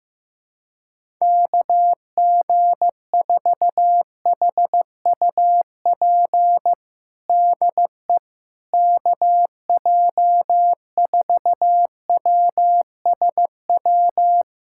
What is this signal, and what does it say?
Morse code, heard as a keyed tone, about 16 wpm, 700 Hz. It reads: KG4HUP DE KJ4WSW
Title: USA Callsign DE Callsign - 6 Characters - 15wpm